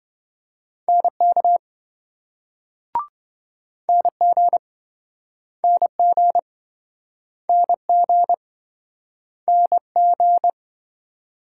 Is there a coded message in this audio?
Yes: Morse code from 0.9 to 10.5 s, heard as a keyed tone, about 27 wpm, 700 Hz. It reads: NK NG NG NG NG